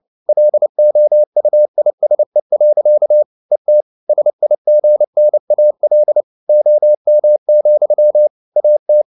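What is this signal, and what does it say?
Morse code, heard as a keyed tone, about 29 wpm, 600 Hz. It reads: LOUISE. ET SIGNAL OM, AT
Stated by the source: Fast (QRQ) Morse at 29 wpm, from CW med Gnister.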